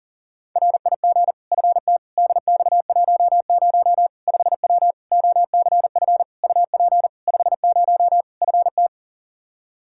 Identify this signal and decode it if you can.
Morse code, heard as a keyed tone, about 40 wpm, 700 Hz. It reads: RIG FT DX10 5W OCF UP 50 FT
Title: Practice QSO - Exchange 2 - Info - 40wpm